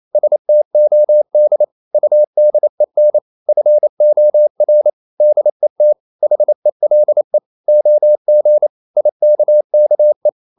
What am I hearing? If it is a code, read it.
Morse code, heard as a keyed tone, about 28 wpm, 600 Hz. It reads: STOD UDEN FOR DET HELE OG IKKE
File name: Ferie QRQ 28wpm